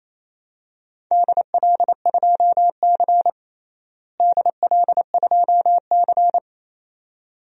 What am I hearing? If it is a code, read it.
Morse code, heard as a keyed tone, about 27 wpm, 700 Hz. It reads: DL2C DL2C
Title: DXCC-100 Call Signs - 4 Characters - Repeated 2x - 28wpm